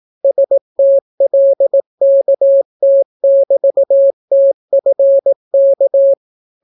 Morse code, heard as a keyed tone, about 18 wpm, 550 Hz. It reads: STLKT<BT>TFK